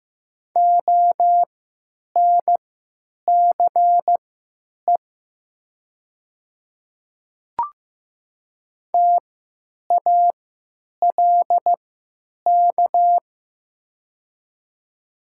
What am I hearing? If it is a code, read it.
Morse code, heard as a keyed tone, about 15 wpm, spaced as for 9 wpm, 700 Hz. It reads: ONCE TALK
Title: 3x Character Spacing - Top 500 Words - 15wpm